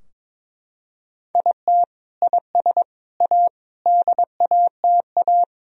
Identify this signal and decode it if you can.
Morse code, heard as a keyed tone, about 22 wpm, 700 Hz. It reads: IT IS A DATA
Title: Sentences from Top 1000 Words - Review Part 3 - 22wpm